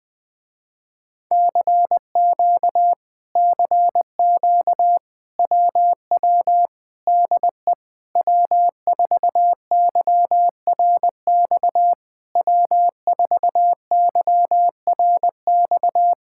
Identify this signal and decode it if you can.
Morse code, heard as a keyed tone, about 20 wpm, 700 Hz. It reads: CQ CQ WW DE W4YRX W4YRX
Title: Speed-Racing Common QSO Phrases - 20wpm